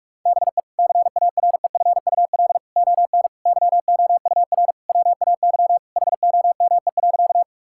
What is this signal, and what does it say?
Morse code, heard as a keyed tone, about 41 wpm, 700 Hz. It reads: BE CAREFUL ON YOUR WAY HOME.
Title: Sentences from Top 1600 Words 45wpm